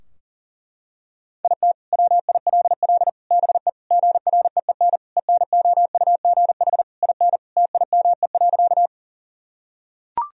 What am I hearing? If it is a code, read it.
Morse code, heard as a keyed tone, about 37 wpm, 700 Hz. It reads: IT WILL BE GREEN ENOUGH IN TIME.
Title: Sentences from Top 1000 Words - Review Part 2 - 40wpm